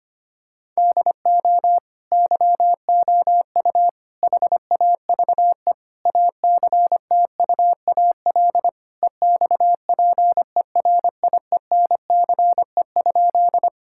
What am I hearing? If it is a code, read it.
Morse code, heard as a keyed tone, about 25 wpm, 700 Hz. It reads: DO YOU HAVE ACTUAL EXPERIENCE?
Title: Sentences from Top 1000 Words - Review Part 5 - 25wpm